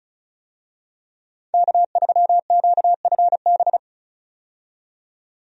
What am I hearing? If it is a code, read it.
Morse code, heard as a keyed tone, about 36 wpm, 700 Hz. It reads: K3QFB
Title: USA Call Signs - 5 Characters - 35wpm